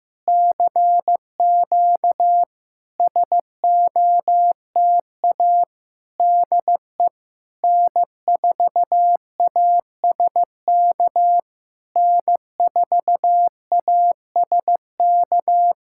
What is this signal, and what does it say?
Morse code, heard as a keyed tone, about 15 wpm, 690 Hz. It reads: CQ SOTA DE N4ASK N4ASK
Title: SOTA QSO Practice 15wpm